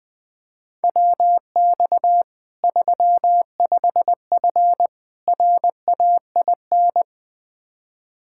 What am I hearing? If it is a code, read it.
Morse code, heard as a keyed tone, about 20 wpm, 700 Hz. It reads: WX 35F RAIN
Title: K1USN SST QSO Practice 20wpm